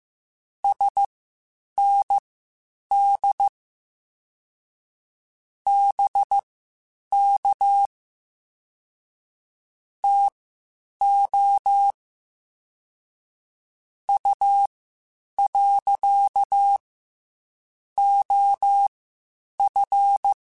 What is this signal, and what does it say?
Morse code, heard as a keyed tone, about 15 wpm, spaced as for 8 wpm, 800 Hz. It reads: SND BK TO U. OF
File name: (spaced) / QSOs Spaced-mp3s / WA7QOX 1 @13 sp